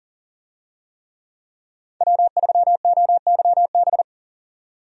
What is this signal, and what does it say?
Morse code, heard as a keyed tone, about 40 wpm, 700 Hz. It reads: W3OYB
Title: USA Call Signs - 5 Characters - 40wpm